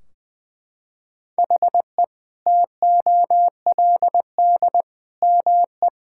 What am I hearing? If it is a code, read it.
Morse code, heard as a keyed tone, about 20 wpm, 700 Hz. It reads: HE TOLD ME
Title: Sentences from Top 1600 Words 20wpm